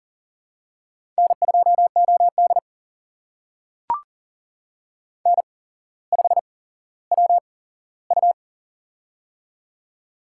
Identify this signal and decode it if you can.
Morse code, heard as a keyed tone, about 38 wpm, spaced as for 17 wpm, 700 Hz. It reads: N2OD N5WU
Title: USA Call Signs - 4 Characters - 8x Character Spacing - 40wpm